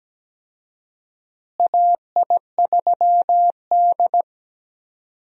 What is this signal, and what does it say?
Morse code, heard as a keyed tone, about 17 wpm, 700 Hz. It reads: AI3D
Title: USA Call Signs - 4 Characters - Repeated 3x - 17wpm